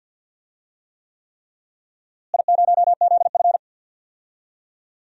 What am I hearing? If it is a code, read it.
Morse code, heard as a keyed tone, about 50 wpm, 700 Hz. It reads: I0ZF
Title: DXCC-100 Call Signs - 4 Characters - Repeated 3x - 50wpm